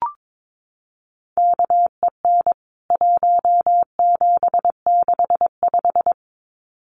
Morse code, heard as a keyed tone, about 22 wpm, 700 Hz. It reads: KEN 1765
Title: CWT Exchange 22wpm